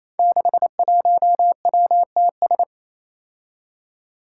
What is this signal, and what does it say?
Morse code, heard as a keyed tone, about 28 wpm, 700 Hz. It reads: 61WTS
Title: Five Challenging Letter-Number Combinations 28wpm